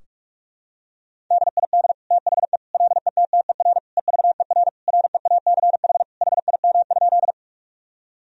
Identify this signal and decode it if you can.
Morse code, heard as a keyed tone, about 46 wpm, 700 Hz. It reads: DID THE LETTER EVER REACH HIM?
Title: Sentences from Top 1000 Words - Review Part 2 - 45wpm